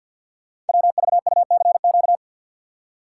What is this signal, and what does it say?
Morse code, heard as a keyed tone, about 48 wpm, 690 Hz. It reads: WVUCQ
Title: Code Groups - 5 Letters - 50wpm